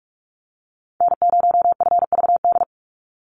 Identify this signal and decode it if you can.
Morse code, heard as a keyed tone, about 44 wpm, 700 Hz. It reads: N0FVD
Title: USA Call Signs 45wpm